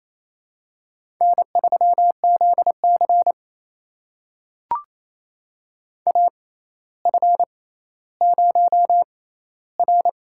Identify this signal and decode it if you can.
Morse code, heard as a keyed tone, about 27 wpm, spaced as for 14 wpm, 700 Hz. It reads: N3ZC AF0R
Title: USA Call Signs - 4 Characters - 6x Character Spacing - 28wpm